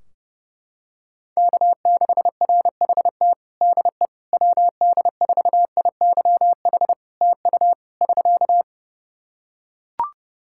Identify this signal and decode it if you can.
Morse code, heard as a keyed tone, about 30 wpm, 700 Hz. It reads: K6RHT DE WD4IYH TU <SK>
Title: Practice QSO - Exchange 4 - Ending - 30wpm